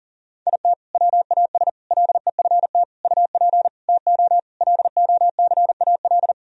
Morse code, heard as a keyed tone, about 40 wpm, 700 Hz. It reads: IT WAS LEFT UP TO LOCAL
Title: Sentences from Top 1000 Words - Review Part 4 - 40wpm